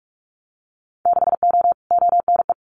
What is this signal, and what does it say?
Morse code, heard as a keyed tone, about 46 wpm, 690 Hz. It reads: 6O ONE
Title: Winter Field Day Exchanges Encore 45wpm